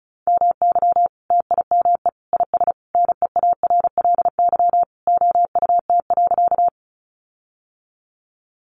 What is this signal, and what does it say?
Morse code, heard as a keyed tone, about 34 wpm, 700 Hz. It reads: MY TIME IS NEARLY OUT.
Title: Sentences from Top 1300 Words 35wpm